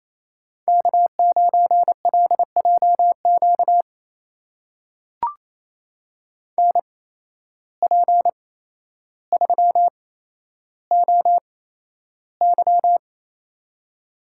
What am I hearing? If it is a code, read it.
Morse code, heard as a keyed tone, about 27 wpm, spaced as for 12 wpm, 700 Hz. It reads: K9LJQ NP3OY
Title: USA Call Signs - 5 Characters - 8x Character Spacing - 28wpm